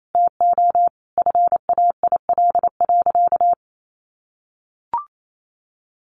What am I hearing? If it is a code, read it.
Morse code, heard as a keyed tone, about 28 wpm, 700 Hz. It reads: TO FAIL.
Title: Sentences from Top 1100 Words 28wpm